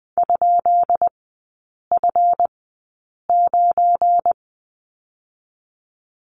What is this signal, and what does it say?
Morse code, heard as a keyed tone, about 20 wpm, spaced as for 8 wpm, 700 Hz. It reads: ?F9
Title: 191 Lesson 31e - Letter v - Three Characters 20wpm